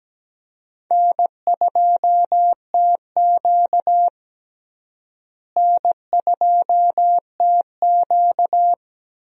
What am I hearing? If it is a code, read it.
Morse code, heard as a keyed tone, about 17 wpm, 700 Hz. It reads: N2TQ N2TQ
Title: USA Call Signs - 4 Characters - Repeated 2x - 17wpm